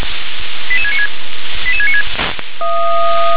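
Five-tone selective call (ZVEI), heard as quick runs of five tones, tones of about 70 ms. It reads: 88696 88696
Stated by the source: Five-tone ZVEI selective call received by SDR.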